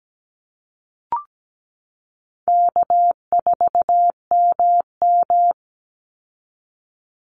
Morse code, heard as a keyed tone, about 17 wpm, 700 Hz. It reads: K4MM
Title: Easy Call Signs Repeated 3x - 17wpm